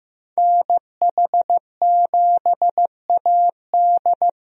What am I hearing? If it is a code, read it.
Morse code, heard as a keyed tone, about 15 wpm, 700 Hz. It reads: NH7AD
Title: USA Call Signs - 5 Characters - Repeated 3x - 15wpm